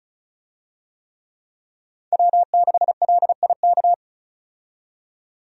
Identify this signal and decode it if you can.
Morse code, heard as a keyed tone, about 34 wpm, 700 Hz. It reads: W6LIK
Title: USA Call Signs - 5 Characters - 35wpm